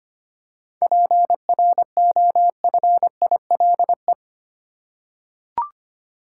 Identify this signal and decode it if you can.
Morse code, heard as a keyed tone, about 25 wpm, 700 Hz. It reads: PROFILE
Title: Seven Letter Words - 25wpm